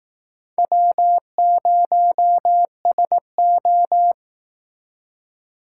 Morse code, heard as a keyed tone, about 19 wpm, 700 Hz. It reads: W0SO
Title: Speed Racing - USA 1x2 Call Signs 15wpm